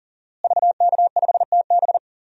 Morse code, heard as a keyed tone, about 40 wpm, 700 Hz. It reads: VK5TB
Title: DXCC-100 Call Signs - 5 Characters - Repeated 3x - 40wpm